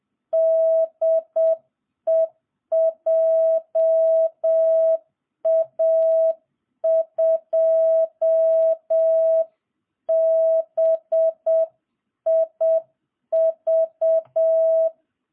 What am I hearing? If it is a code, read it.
Morse code, heard as a keyed tone, about 7 wpm, 660 Hz. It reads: DE JA2BIV